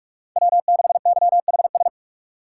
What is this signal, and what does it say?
Morse code, heard as a keyed tone, about 41 wpm, 700 Hz. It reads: W6YHS
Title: USA Call Signs - 6x Character Spacing - 45wpm